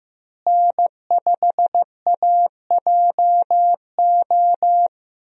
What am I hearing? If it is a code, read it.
Morse code, heard as a keyed tone, about 15 wpm, 700 Hz. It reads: N5AJO
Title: USA Call Signs - 5 Characters - Repeated 3x - 15wpm